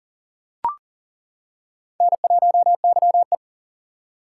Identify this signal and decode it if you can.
Morse code, heard as a keyed tone, about 40 wpm, 700 Hz. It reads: N1YE
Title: K1USN SST QSO Practice - 40wpm